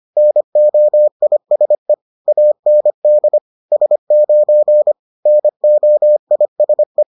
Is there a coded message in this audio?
Yes: Morse code from 0.2 to 7.0 s, heard as a keyed tone, about 25 wpm, 600 Hz. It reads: NOISE AND S9 NOISE